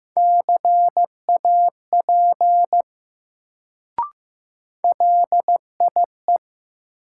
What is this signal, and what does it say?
Morse code, heard as a keyed tone, about 15 wpm, 700 Hz. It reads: CAP LIE
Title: Sentences from Top 1600 Words - New Word Drill Alt - 15wpm